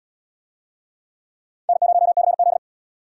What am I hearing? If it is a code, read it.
Morse code, heard as a keyed tone, about 77 wpm, 700 Hz. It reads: N0CZ